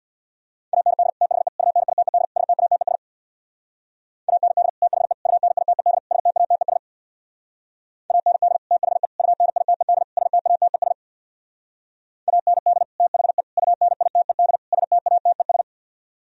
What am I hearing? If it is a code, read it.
Morse code, heard as a keyed tone, about 63 wpm, 700 Hz. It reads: AND THE UNITED STATES AND THE UNITED STATES AND THE UNITED STATES AND THE UNITED STATES